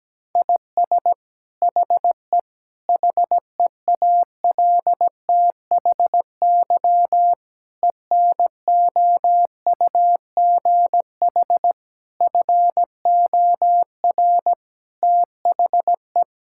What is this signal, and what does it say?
Morse code, heard as a keyed tone, about 17 wpm, 700 Hz. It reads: IS HE HEALTHY ENOUGH FOR THE
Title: Sentences from Top 1000 Words - Review Part 5 - 17wpm